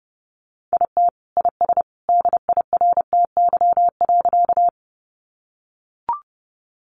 Morse code, heard as a keyed tone, about 30 wpm, 700 Hz. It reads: IT IS DIRTY.
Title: Additional Easy Phrases 30wpm